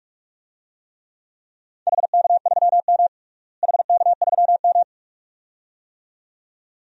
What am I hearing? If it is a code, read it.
Morse code, heard as a keyed tone, about 42 wpm, 700 Hz. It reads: HK3M HK3M
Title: DXCC-100 Call Signs - 4 Characters - Repeated 2x - 45wpm